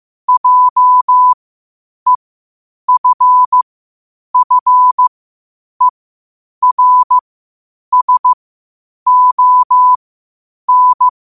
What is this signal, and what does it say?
Morse code, heard as a keyed tone, about 15 wpm, spaced as for 8 wpm, 1000 Hz. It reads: JEFFERSON
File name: qotdfiles / 8wpm